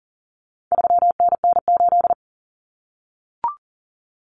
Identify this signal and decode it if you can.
Morse code, heard as a keyed tone, about 40 wpm, 700 Hz. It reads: 3NN8
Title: Common Cut Numbers - 4-Digit 40wpm